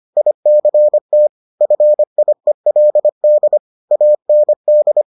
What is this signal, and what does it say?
Morse code, heard as a keyed tone, about 25 wpm, 600 Hz. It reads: ICT FIELD AND